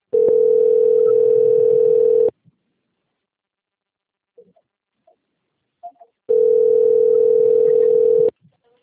Ringback tone (North America), 440 + 480 Hz.